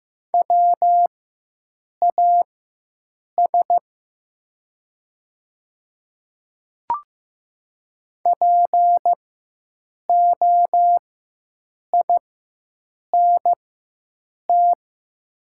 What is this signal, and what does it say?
Morse code, heard as a keyed tone, about 15 wpm, spaced as for 7 wpm, 690 Hz. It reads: WAS POINT
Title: 4x Character Spacing - Top 200 Words - 15wpm